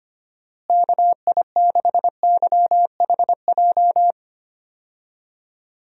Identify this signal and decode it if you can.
Morse code, heard as a keyed tone, about 25 wpm, 700 Hz. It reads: KI6YHJ
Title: USA Call Signs - 6 Characters - Repeated 2x - 25wpm